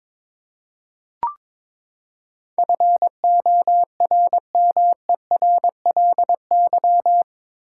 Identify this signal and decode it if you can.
Morse code, heard as a keyed tone, about 22 wpm, 700 Hz. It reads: FORMERLY
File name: practice / 22wpm / Eight Letter Words - 22wpm